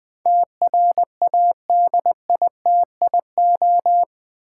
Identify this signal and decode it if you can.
Morse code, heard as a keyed tone, about 20 wpm, 700 Hz. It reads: TRADITIO
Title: Build A Word - 4 Letters At A Time - 11-Letter Words - 20wpm